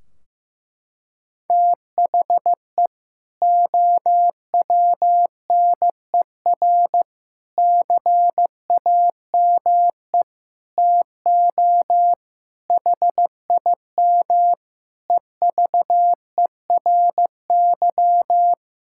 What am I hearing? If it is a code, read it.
Morse code, heard as a keyed tone, about 15 wpm, 700 Hz. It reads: THE OWNER CAME TO HIM EVERY
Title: Sentences from Top 1600 Words 15wpm